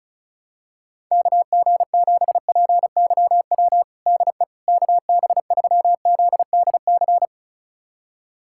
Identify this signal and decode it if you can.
Morse code, heard as a keyed tone, about 34 wpm, 700 Hz. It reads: KG7PYW DE KB3ZDC
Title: USA Callsign DE Callsign - 6 Characters - 35wpm